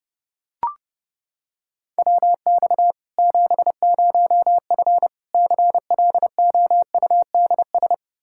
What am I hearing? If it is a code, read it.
Morse code, heard as a keyed tone, about 30 wpm, 700 Hz. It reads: WX 70F CLOUDS
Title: Practice QSO - Exchange 2 - Info - 30wpm